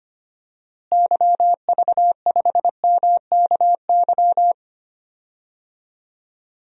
Morse code, heard as a keyed tone, about 25 wpm, 700 Hz. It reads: YV5MKY
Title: DXCC-100 Call Signs - 6 Characters - Repeated 3x - 25wpm